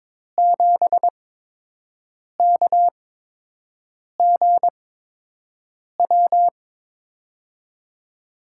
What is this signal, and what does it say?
Morse code, heard as a keyed tone, about 23 wpm, spaced as for 6 wpm, 700 Hz. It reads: 7KGW
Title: USA Call Signs - 8x Character Spacing - 22wpm